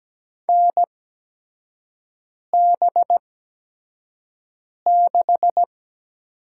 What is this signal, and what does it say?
Morse code, heard as a keyed tone, about 17 wpm, spaced as for 5 wpm, 700 Hz. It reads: NB6
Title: USA Call Signs - 4 Characters - 8x Character Spacing - 17wpm